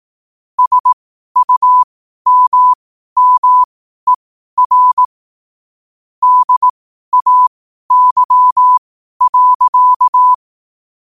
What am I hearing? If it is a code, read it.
Morse code, heard as a keyed tone, about 18 wpm, spaced as for 12 wpm, 1000 Hz. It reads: SUMMER DAY.
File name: qotdfiles / 12wpm